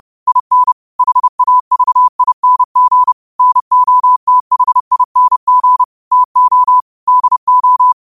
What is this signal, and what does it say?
Morse code, heard as a keyed tone, about 30 wpm, 1000 Hz. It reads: IN HAVING NOTHING TO DO
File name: qotdfiles / 30wpm